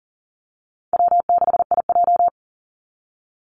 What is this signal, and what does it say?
Morse code, heard as a keyed tone, about 40 wpm, 700 Hz. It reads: W6IJ
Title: Call Signs - 40wpm